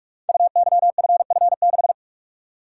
Morse code, heard as a keyed tone, about 44 wpm, 700 Hz. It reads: UYFF6
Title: Code Groups - 5 Letter:Numbers - 45wpm